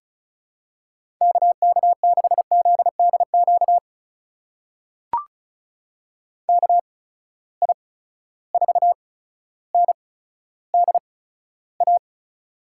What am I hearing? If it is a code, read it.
Morse code, heard as a keyed tone, about 36 wpm, spaced as for 15 wpm, 700 Hz. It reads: KK6ZDQ KI4NDA
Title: USA Call Signs - 6 Characters - 8x Character Spacing - 35wpm